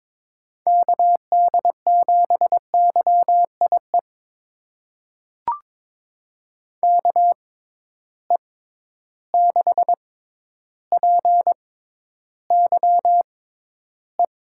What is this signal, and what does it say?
Morse code, heard as a keyed tone, about 22 wpm, spaced as for 11 wpm, 700 Hz. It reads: KD7YIE KE6PYE
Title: USA Call Signs - 6 Characters - 6x Character Spacing - 22wpm